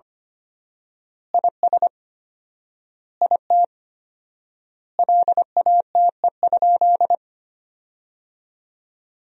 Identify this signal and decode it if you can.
Morse code, heard as a keyed tone, about 25 wpm, 700 Hz. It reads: IS IT LATE?
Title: 4x Word-Spacing - Sentences from Top 300 Words - 25wpm